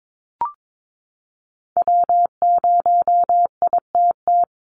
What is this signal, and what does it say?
Morse code, heard as a keyed tone, about 22 wpm, 700 Hz. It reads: W0ITT
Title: Call Signs Repeated 2x - 22wpm